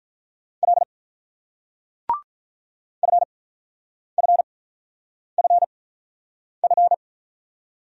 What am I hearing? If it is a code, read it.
Morse code, heard as a keyed tone, about 46 wpm, spaced as for 9 wpm, 700 Hz. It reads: L FFFF